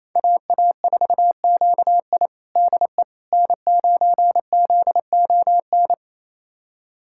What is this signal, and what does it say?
Morse code, heard as a keyed tone, about 28 wpm, 700 Hz. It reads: AA4QI DE N9ZON
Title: USA Callsign DE Callsign - 5 Characters - 28wpm